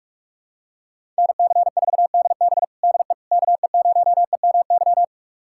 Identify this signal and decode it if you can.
Morse code, heard as a keyed tone, about 46 wpm, 700 Hz. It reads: NC4DB DE KE0EMY
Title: USA Callsign DE Callsign 45wpm